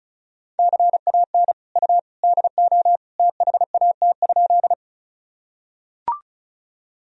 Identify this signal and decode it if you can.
Morse code, heard as a keyed tone, about 36 wpm, 700 Hz. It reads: CAN U DO THAT?
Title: Sentences from Top 100 Words and Abbreviations 35wpm